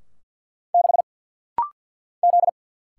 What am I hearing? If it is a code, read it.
Morse code, heard as a keyed tone, about 49 wpm, 700 Hz. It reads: 6 Z
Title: Single Letter-Number - Rapid-Fire 50wpm